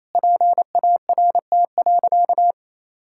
Morse code, heard as a keyed tone, about 28 wpm, 700 Hz. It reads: PART.
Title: Sentences from Top 1000 Words - Review Part 6 - 28wpm